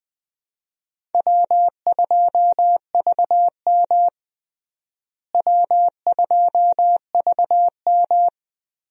Morse code, heard as a keyed tone, about 20 wpm, 700 Hz. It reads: W2VM W2VM
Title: USA Call Signs - 4 Characters - Repeated 3x - 20wpm